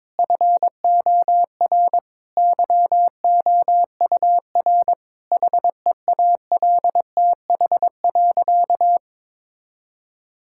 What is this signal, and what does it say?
Morse code, heard as a keyed tone, about 22 wpm, 700 Hz. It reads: FOR YOUR HEALTH.
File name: Difficult Sentences with Hints - 22wpm